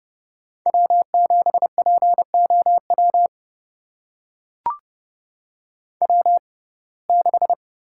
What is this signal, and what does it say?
Morse code, heard as a keyed tone, about 29 wpm, spaced as for 22 wpm, 700 Hz. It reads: W7POW W6
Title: USA Call Signs - 5 Characters - 6x Character Spacing - 30wpm